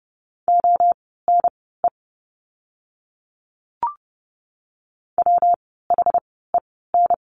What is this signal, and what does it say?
Morse code, heard as a keyed tone, about 30 wpm, spaced as for 17 wpm, 700 Hz. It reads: ONE WHEN
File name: course / 30:10wpm / 115 Lesson 19f - Letter w - Words 30wpm